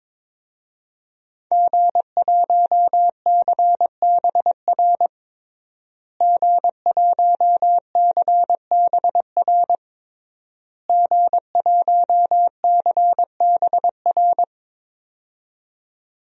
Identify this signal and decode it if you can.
Morse code, heard as a keyed tone, about 22 wpm, 700 Hz. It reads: G1CBR G1CBR G1CBR
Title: DXCC-100 Call Signs - 5 Characters - Repeated 3x - 22wpm